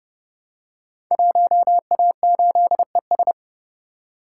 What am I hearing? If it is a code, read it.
Morse code, heard as a keyed tone, about 30 wpm, 700 Hz. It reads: 1A8ES
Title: Code Groups - 5 Letter:Numbers - 30wpm